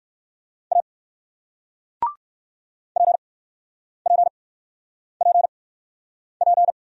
Morse code, heard as a keyed tone, about 55 wpm, spaced as for 9 wpm, 700 Hz. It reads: A PPPP
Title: Speed-Racing Single Letter-Number 45wpm